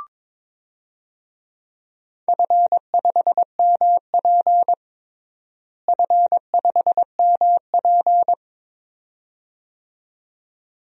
Morse code, heard as a keyed tone, about 22 wpm, 700 Hz. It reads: F5MP F5MP
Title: DXCC-100 Call Signs - 4 Characters - Repeated 2x - 22wpm